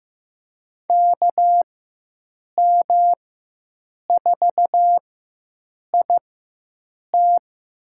Morse code, heard as a keyed tone, about 15 wpm, spaced as for 7 wpm, 700 Hz. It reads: KM4IT
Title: USA Call Signs - 5 Characters - 4x Character Spacing - 15wpm